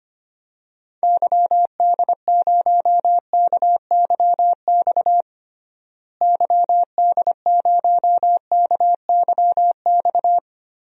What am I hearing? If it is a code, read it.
Morse code, heard as a keyed tone, about 25 wpm, 700 Hz. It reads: YD0KYX YD0KYX
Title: DXCC-100 Call Signs - 6 Characters - Repeated 2x - 25wpm